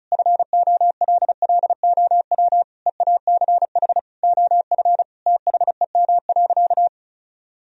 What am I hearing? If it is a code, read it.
Morse code, heard as a keyed tone, about 34 wpm, 700 Hz. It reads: FOLLOW EACH OF THEM.
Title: Sentences from Top 1000 Words - Review Part 1 - 35wpm